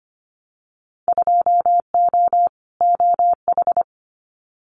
Morse code, heard as a keyed tone, about 25 wpm, 700 Hz. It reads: 2O OH
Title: Winter Field Day Exchanges 25wpm